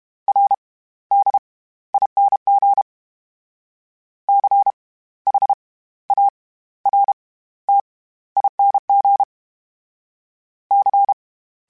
Morse code, heard as a keyed tone, about 32 wpm, spaced as for 16 wpm, 800 Hz. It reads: RDING CHARTING C
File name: Special INGs / Special INGs-mp3s / SlowToFaster ing @30